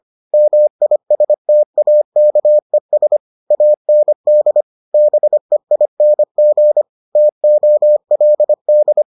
Morse code, heard as a keyed tone, about 25 wpm, 600 Hz. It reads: MISTAKES AND BEING TOLD